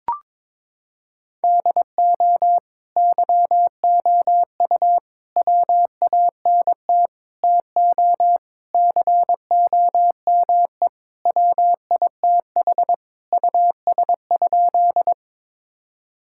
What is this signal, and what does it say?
Morse code, heard as a keyed tone, about 22 wpm, 700 Hz. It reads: DO YOU WANT TO COME WITH US?
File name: practice / 22wpm / Sentences with 3-Word Phrases - 22wpm